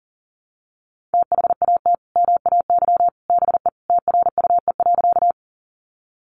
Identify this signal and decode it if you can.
Morse code, heard as a keyed tone, about 37 wpm, 700 Hz. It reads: THAT MAY BE TRUE.
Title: Sentences from Top 400 Words 40wpm